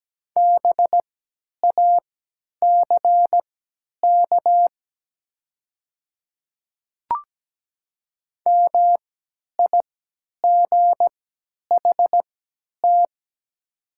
Morse code, heard as a keyed tone, about 17 wpm, spaced as for 10 wpm, 700 Hz. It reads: BACK MIGHT
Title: 3x Character Spacing - Top 500 Words - 17wpm